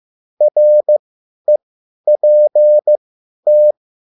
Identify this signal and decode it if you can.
Morse code, heard as a keyed tone, about 15 wpm, spaced as for 10 wpm, 600 Hz. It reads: REPT
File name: QSL / CW lesson